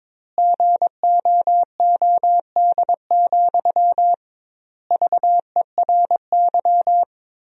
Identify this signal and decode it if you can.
Morse code, heard as a keyed tone, about 22 wpm, 700 Hz. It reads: GOOD, VERY
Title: 2x Word-Spacing - Sentences from Top 200 Words - 22wpm